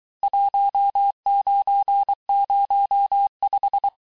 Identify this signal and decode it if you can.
Morse code, heard as a keyed tone, about 23 wpm, 800 Hz. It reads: 1905
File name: cours-du-lundi files / cours-1852-texte-clair-vitesse-17MPM-puis-22MPM